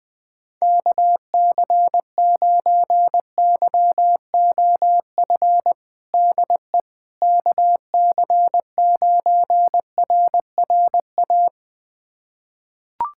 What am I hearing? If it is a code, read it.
Morse code, heard as a keyed tone, about 20 wpm, 700 Hz. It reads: KC9YOF DE KC9RRA
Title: USA Callsign DE Callsign 20wpm